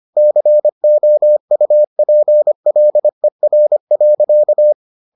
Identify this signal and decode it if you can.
Morse code, heard as a keyed tone, about 25 wpm, 600 Hz. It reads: COUPLER.